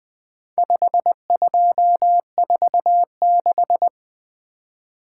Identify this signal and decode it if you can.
Morse code, heard as a keyed tone, about 20 wpm, 700 Hz. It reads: 5246
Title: Four Letter-Number Combinations - 20wpm